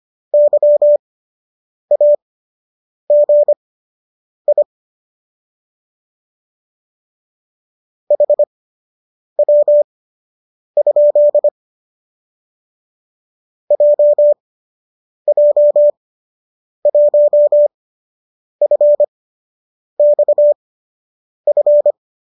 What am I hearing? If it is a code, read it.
Morse code, heard as a keyed tone, about 25 wpm, spaced as for 8 wpm, 590 Hz. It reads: YAGI HW? JJ1FXF